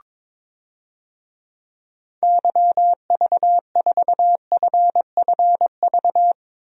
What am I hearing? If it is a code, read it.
Morse code, heard as a keyed tone, about 22 wpm, 700 Hz. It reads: YV4FFV
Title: DXCC-100 Call Signs - 6 Characters - 22wpm